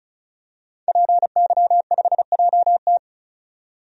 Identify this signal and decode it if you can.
Morse code, heard as a keyed tone, about 36 wpm, 700 Hz. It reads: PY5JT
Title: DXCC-100 Call Signs - 5 Characters - 35wpm